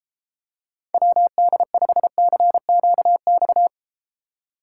Morse code, heard as a keyed tone, about 31 wpm, 700 Hz. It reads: WD5CQX